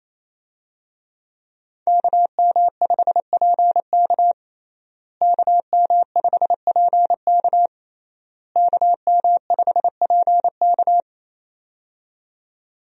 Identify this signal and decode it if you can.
Morse code, heard as a keyed tone, about 28 wpm, 700 Hz. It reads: KM5PK KM5PK KM5PK
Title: USA Call Signs - 5 Characters - Repeated 3x - 28wpm